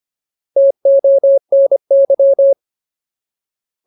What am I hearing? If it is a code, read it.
Morse code, heard as a keyed tone, about 25 wpm, 550 Hz. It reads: TONY
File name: Ultimate Word List / 25WPM Ultimate